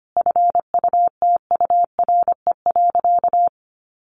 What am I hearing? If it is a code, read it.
Morse code, heard as a keyed tone, about 25 wpm, 700 Hz. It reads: FUTURE.
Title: Sentences from Top 1200 Words 25wpm